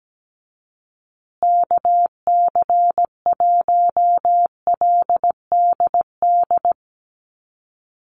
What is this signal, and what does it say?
Morse code, heard as a keyed tone, about 17 wpm, 700 Hz. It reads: KC1LDD
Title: USA Call Signs Repeated 2x 17wpm